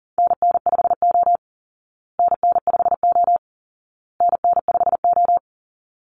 Morse code, heard as a keyed tone, about 40 wpm, 700 Hz. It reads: NN5O NN5O NN5O
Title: Call Signs Repeated 2x - 40wpm